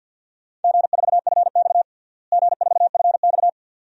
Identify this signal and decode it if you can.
Morse code, heard as a keyed tone, about 52 wpm, 700 Hz. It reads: G4FX G4FX
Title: DXCC-100 Call Signs - 4 Characters - Repeated 2x - 50wpm